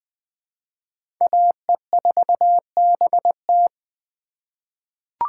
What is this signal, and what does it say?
Morse code, heard as a keyed tone, about 20 wpm, 700 Hz. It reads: AE4BT
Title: USA Call Signs - 5 Characters - 2x Character Spacing - 20wpm